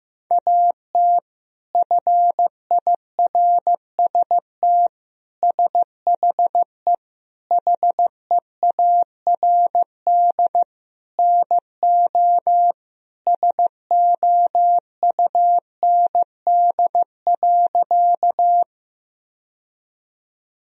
Morse code, heard as a keyed tone, about 15 wpm, 700 Hz. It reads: AT FIRST SHE HEARD NO SOUND.
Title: Sentences from Top 1000 Words - Review Part 1 - 15wpm